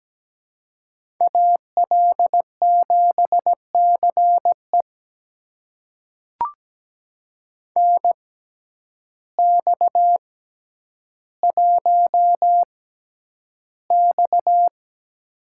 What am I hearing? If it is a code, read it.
Morse code, heard as a keyed tone, about 17 wpm, spaced as for 9 wpm, 700 Hz. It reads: AL7CE NX1X
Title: USA Call Signs - 6x Character Spacing - 17wpm